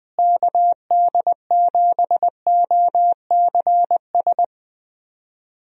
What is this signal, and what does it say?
Morse code, heard as a keyed tone, about 20 wpm, 700 Hz. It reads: KD7OCS
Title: USA Call Signs - 6 Characters - Repeated 3x - 20wpm